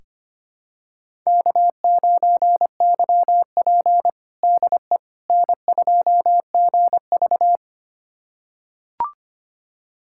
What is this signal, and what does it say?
Morse code, heard as a keyed tone, about 25 wpm, 700 Hz. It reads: K9YP DE N2GV
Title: USA Callsign DE Callsign - 4 Characters - 25wpm